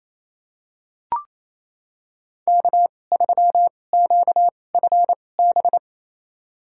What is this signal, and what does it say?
Morse code, heard as a keyed tone, about 28 wpm, spaced as for 20 wpm, 700 Hz. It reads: K3QFB
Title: USA Call Signs - 5 Characters - 2x Character Spacing - 28wpm